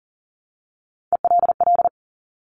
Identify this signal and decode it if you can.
Morse code, heard as a keyed tone, about 38 wpm, 700 Hz. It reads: ELL
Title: Top Three Letter Combinations in Words 40wpm